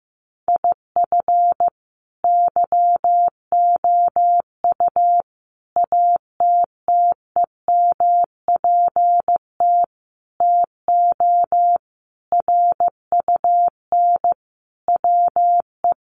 Morse code, heard as a keyed tone, about 15 wpm, 700 Hz. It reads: IF YOU ATTEMPT TO RUN WE
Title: Sentences from Top 1200 Words 15wpm